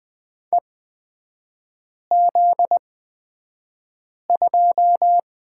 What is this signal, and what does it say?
Morse code, heard as a keyed tone, about 20 wpm, spaced as for 5 wpm, 700 Hz. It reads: EZ2
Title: 250 Lesson 40e - BK - Three Characters 20wpm